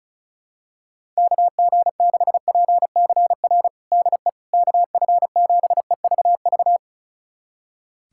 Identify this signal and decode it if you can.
Morse code, heard as a keyed tone, about 34 wpm, 700 Hz. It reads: KG6PCR DE KF7EVV
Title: USA Callsign DE Callsign - 6 Characters - 35wpm